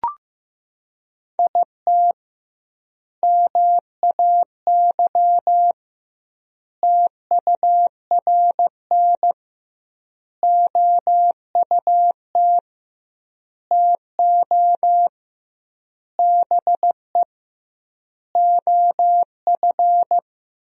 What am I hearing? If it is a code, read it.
Morse code, heard as a keyed tone, about 15 wpm, 700 Hz. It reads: IT MAY TURN OUT TO BE OF
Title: 2x Word-Spacing - Sentences from Top 400 Words - 15wpm